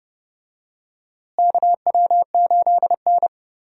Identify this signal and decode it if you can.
Morse code, heard as a keyed tone, about 32 wpm, 700 Hz. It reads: KW8N
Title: USA Call Signs - 4 Characters - Repeated 2x - 30wpm